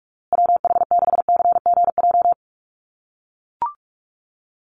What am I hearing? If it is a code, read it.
Morse code, heard as a keyed tone, about 44 wpm, 700 Hz. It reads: WH6CGJ
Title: USA Call Signs Repeated 3x 45wpm